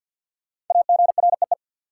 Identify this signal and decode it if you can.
Morse code, heard as a keyed tone, about 53 wpm, 700 Hz. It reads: AGREE
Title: Sentences from Top 1000 Words - Review Part 6 - New Word Drill - 50wpm